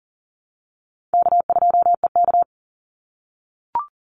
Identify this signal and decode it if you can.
Morse code, heard as a keyed tone, about 40 wpm, 700 Hz. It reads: K2EK
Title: Easy Call Signs - 40wpm